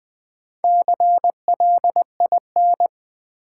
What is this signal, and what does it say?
Morse code, heard as a keyed tone, about 20 wpm, 700 Hz. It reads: CLIN
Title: Build A Word - 1 Letter At A Time - 5-Letter Words - 20wpm